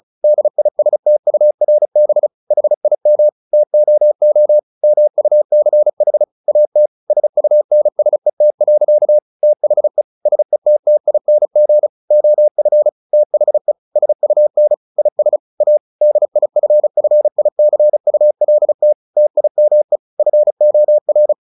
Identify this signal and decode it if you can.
Morse code, heard as a keyed tone, about 34 wpm, 600 Hz. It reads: DISTURB HIM TOO MUCH AT SUNSET. THE SETTING OF THE SUN IS A DIFFICULT TIME FOR
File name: the old man and the sea 2 35wpm